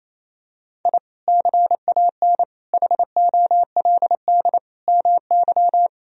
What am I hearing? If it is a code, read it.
Morse code, heard as a keyed tone, about 27 wpm, 700 Hz. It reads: I CAN HOLD MY
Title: Sentences from Top 1000 Words - Review Part 2 - 28wpm